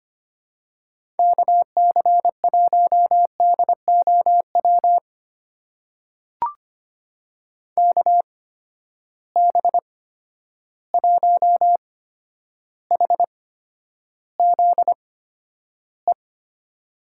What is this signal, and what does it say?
Morse code, heard as a keyed tone, about 25 wpm, spaced as for 11 wpm, 700 Hz. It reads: KC1DOW KB1HZE
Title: USA Call Signs - 6 Characters - 8x Character Spacing - 25wpm